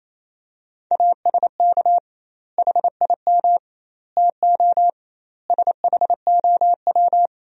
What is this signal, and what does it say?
Morse code, heard as a keyed tone, about 28 wpm, 700 Hz. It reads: ASK HIM TO SHOW
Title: 2x Word-Spacing - Sentences from Top 200 Words - 28wpm